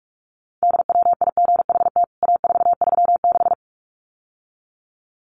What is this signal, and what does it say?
Morse code, heard as a keyed tone, about 44 wpm, 700 Hz. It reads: DWIGHT A436
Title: CWT Exchange 45wpm